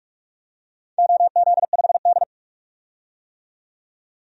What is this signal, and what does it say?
Morse code, heard as a keyed tone, about 44 wpm, 700 Hz. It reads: OZ5D
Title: DXCC-100 Call Signs - 4 Characters - 45wpm